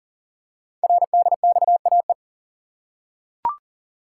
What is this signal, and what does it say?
Morse code, heard as a keyed tone, about 41 wpm, 690 Hz. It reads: RDXAE
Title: Code Groups - 5 Letters - 40wpm